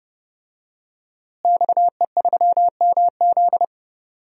Morse code, heard as a keyed tone, about 30 wpm, 700 Hz. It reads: XE3MZ
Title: DXCC-100 Call Signs - 5 Characters - Repeated 2x - 30wpm